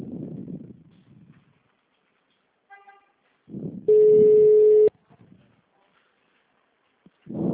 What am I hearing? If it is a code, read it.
Ringback tone (Europe, 425 Hz).
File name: out- -2011- - - .83